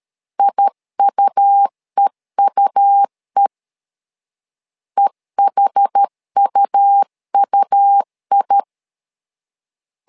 Morse code, heard as a keyed tone, about 13 wpm, 800 Hz. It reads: IUEUE EHUUI
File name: LEZIONI / 2 UHDN / 2 UH 12wpm 1439